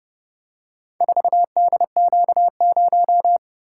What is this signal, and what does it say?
Morse code, heard as a keyed tone, about 29 wpm, 700 Hz. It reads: 4DQ0